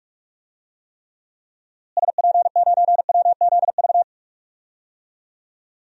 Morse code, heard as a keyed tone, about 48 wpm, 700 Hz. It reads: SP9WZV
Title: DXCC-100 Call Signs - 6 Characters - Repeated 2x - 45wpm